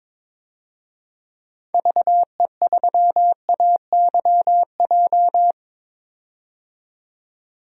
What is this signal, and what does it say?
Morse code, heard as a keyed tone, about 22 wpm, 700 Hz. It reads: VE3AYJ
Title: DXCC-100 Call Signs - 6 Characters - 22wpm